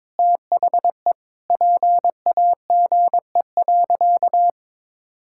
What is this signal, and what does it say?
Morse code, heard as a keyed tone, about 22 wpm, 700 Hz. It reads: THE PAGE.
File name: practice / 22wpm / Sentences with 4-Word Phrases - 22wpm